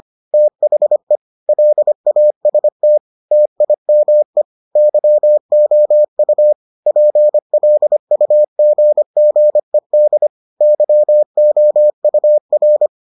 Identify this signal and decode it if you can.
Morse code, heard as a keyed tone, about 25 wpm, 600 Hz. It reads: THE LAST TIME YOU PLUGGED YOUR